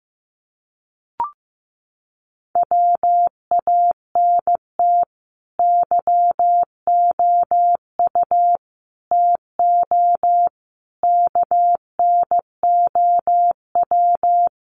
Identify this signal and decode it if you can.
Morse code, heard as a keyed tone, about 15 wpm, 700 Hz. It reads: WANT YOU TO KNOW
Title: Sets of 5 Words Encore 15wpm